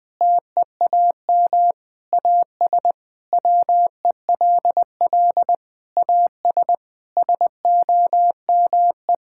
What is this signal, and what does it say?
Morse code, heard as a keyed tone, about 20 wpm, 700 Hz. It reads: TEAM AS WELL AS SOME
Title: Sentences with 6-Word Phrases - Spoken Beginning - 20wpm